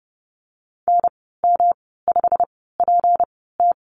Morse code, heard as a keyed tone, about 30 wpm, spaced as for 17 wpm, 700 Hz. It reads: NM5PT
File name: Lesson 37g - Letter x - Call Signs 30wpm